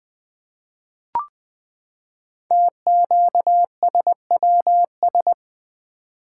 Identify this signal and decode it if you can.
Morse code, heard as a keyed tone, about 20 wpm, 700 Hz. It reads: TQSWS
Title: Code Groups - 5 Letters - 20wpm